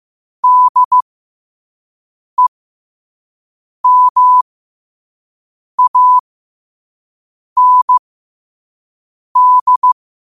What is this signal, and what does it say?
Morse code, heard as a keyed tone, about 15 wpm, spaced as for 5 wpm, 1000 Hz. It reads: DEMAND
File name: qotdfiles / 5wpm